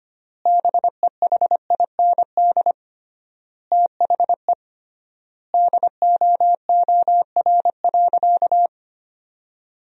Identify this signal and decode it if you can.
Morse code, heard as a keyed tone, about 25 wpm, 700 Hz. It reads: BEHIND THE DOOR.
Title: 3x Word-Spacing - Sentences from Top 500 Words - 25wpm